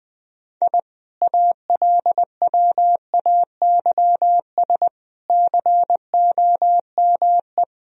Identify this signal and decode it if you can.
Morse code, heard as a keyed tone, about 20 wpm, 700 Hz. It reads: I ALWAYS COME
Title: Sentences from Top 1000 Words - Review Part 2 - 20wpm